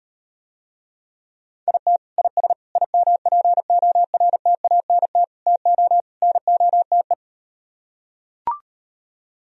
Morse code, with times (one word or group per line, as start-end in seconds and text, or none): 1.7-2.0 IT
2.2-2.5 IS
2.8-5.2 IMPORTANT
5.5-6.0 TO
6.2-7.1 NOTE